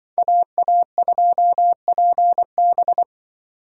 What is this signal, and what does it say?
Morse code, heard as a keyed tone, about 24 wpm, 700 Hz. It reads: AA2PB
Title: Speed Racing - USA 2x2 Call Signs 20wpm